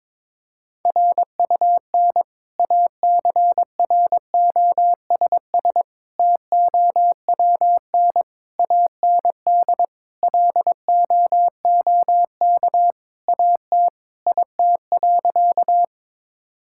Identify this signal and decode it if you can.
Morse code, heard as a keyed tone, about 21 wpm, 700 Hz. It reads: RUN ACROSS TOWN AND LOOK AT IT.
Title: Sentences from Top 1000 Words - Review Part 5 - 22wpm